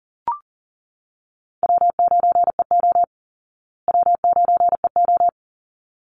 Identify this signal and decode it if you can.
Morse code, heard as a keyed tone, about 40 wpm, 700 Hz. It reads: W9EO W9EO
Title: Easy Call Signs Repeated 2x - 40wpm